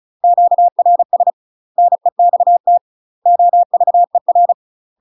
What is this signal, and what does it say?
Morse code, heard as a keyed tone, about 35 wpm, 700 Hz. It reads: QRS NEXT OVER
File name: UK MASTER QSOs TRAINING / 35WPM / UK MASTER QSOs 0039